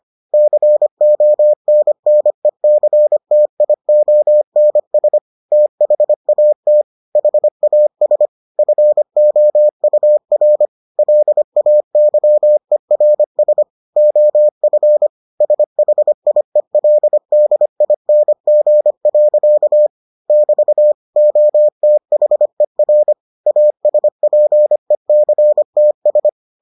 Morse code, heard as a keyed tone, about 25 wpm, 600 Hz. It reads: CONNECTIONS THAT HAS FOUR LAYERS OF SHIELDING. <BT> OTHER ASPECTS